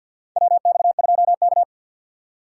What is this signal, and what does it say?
Morse code, heard as a keyed tone, about 50 wpm, 700 Hz. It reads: WX2K
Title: USA Call Signs - 6x Character Spacing - 50wpm